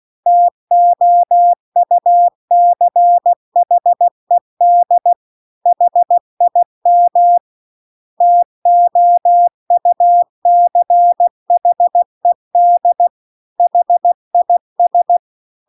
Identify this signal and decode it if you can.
Morse code, heard as a keyed tone, about 16 wpm, 700 Hz. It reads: TOUCHED HIM TOUCHED HIS
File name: CW-The-Big-Twoheaded-River-2-16WPM